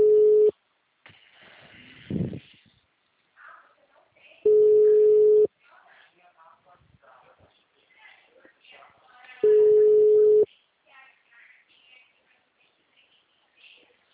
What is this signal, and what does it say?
Ringback tone (Europe, 425 Hz), with speech.